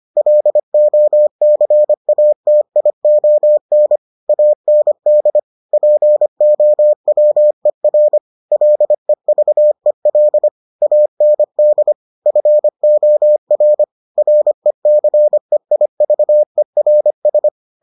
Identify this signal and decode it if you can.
Morse code, heard as a keyed tone, about 25 wpm, 600 Hz. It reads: LOCATION AND POWER LEVEL AND FOR RECEIVERS